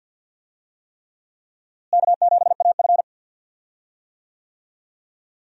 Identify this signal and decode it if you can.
Morse code, heard as a keyed tone, about 46 wpm, 700 Hz. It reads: K7AF
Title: USA Call Signs - 4 Characters - Repeated 3x - 50wpm